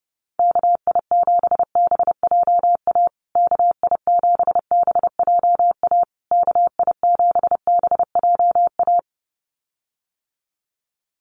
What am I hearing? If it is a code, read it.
Morse code, heard as a keyed tone, about 30 wpm, 700 Hz. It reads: KI7BJA KI7BJA KI7BJA
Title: USA Call Signs Repeated 3x 30wpm